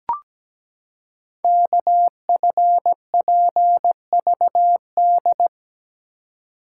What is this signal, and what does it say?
Morse code, heard as a keyed tone, about 17 wpm, 700 Hz. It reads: KFPVD
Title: Code Groups - 5 Letters - 17wpm